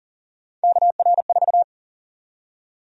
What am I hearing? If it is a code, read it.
Morse code, heard as a keyed tone, about 40 wpm, 700 Hz. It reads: KR4
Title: USA Call Sign Prefixes - Repeated 2x - 40wpm